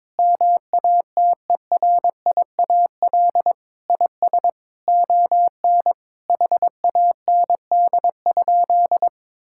Morse code, heard as a keyed tone, about 22 wpm, 700 Hz. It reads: MATERIAL IS ON HAND?
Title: Sentences from Top 1000 Words - Review Part 5 - 22wpm